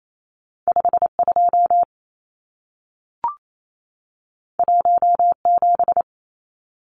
Morse code, heard as a keyed tone, about 28 wpm, 700 Hz. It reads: 52 17
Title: Numbers - 2-Digit 28wpm